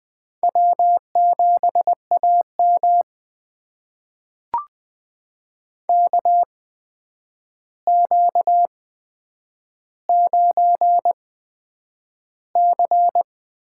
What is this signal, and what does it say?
Morse code, heard as a keyed tone, about 20 wpm, spaced as for 9 wpm, 700 Hz. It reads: W7AM KQ9C
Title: USA Call Signs - 4 Characters - 8x Character Spacing - 20wpm